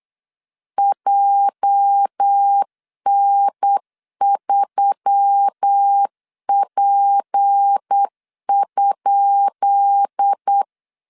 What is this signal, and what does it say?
Morse code, heard as a keyed tone, about 8 wpm, 800 Hz. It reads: JN3P?
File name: LEZIONI / 10 SIGNS / 10B INT 8wpm 3534